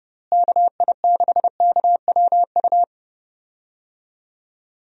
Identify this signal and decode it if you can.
Morse code, heard as a keyed tone, about 32 wpm, 700 Hz. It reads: KI6KWU
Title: USA Call Signs - 6 Characters - Repeated 2x - 30wpm